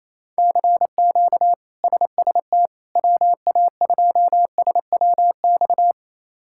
Morse code, heard as a keyed tone, about 28 wpm, 700 Hz. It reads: CQ SST WA2SWX
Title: K1USN SST QSO Practice - 28wpm